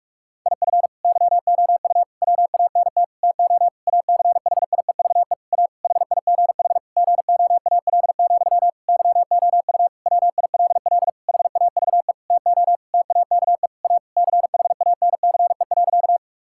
Morse code, heard as a keyed tone, about 46 wpm, 700 Hz. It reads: IF YOU WANT TO ACHIEVE A HIGH GOAL, YOU WILL HAVE TO TAKE A CHANCE.
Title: Sentences from Top 1000 Words - Review Part 6 - 45wpm